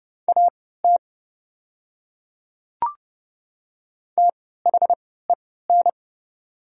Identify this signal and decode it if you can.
Morse code, heard as a keyed tone, about 30 wpm, spaced as for 17 wpm, 700 Hz. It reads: AT THEN
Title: 3x Character Spacing - Top 100 Words - 30wpm